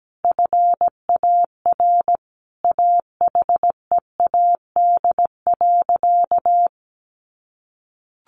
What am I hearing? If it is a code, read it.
Morse code, heard as a keyed tone, about 17 wpm, 700 Hz. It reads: FAR AHEAD.
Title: Sentences from Top 1100 Words 17wpm